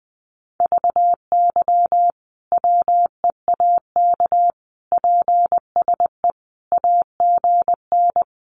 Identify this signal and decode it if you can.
Morse code, heard as a keyed tone, about 20 wpm, 700 Hz. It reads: VY WEAK PSE AGN
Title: Common QSO Phrases 20wpm